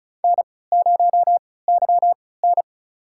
Morse code, heard as a keyed tone, about 36 wpm, spaced as for 20 wpm, 700 Hz. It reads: N0YN
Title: USA Call Signs - 3x Character Spacing - 35wpm